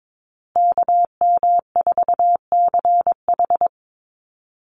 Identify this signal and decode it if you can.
Morse code, heard as a keyed tone, about 22 wpm, 700 Hz. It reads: KM4CH
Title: Call Signs Repeated 2x - 22wpm